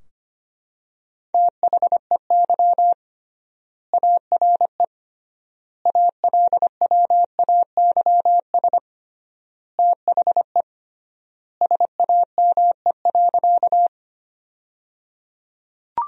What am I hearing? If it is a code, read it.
Morse code, heard as a keyed tone, about 25 wpm, 700 Hz. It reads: THEY ARE ALWAYS THE SAME.
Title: 3x Word-Spacing - Sentences from Top 300 Words - 25wpm